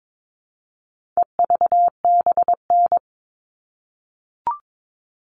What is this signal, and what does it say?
Morse code, heard as a keyed tone, about 22 wpm, 700 Hz. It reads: EVBN
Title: All Cut Numbers - 4-Digit 22wpm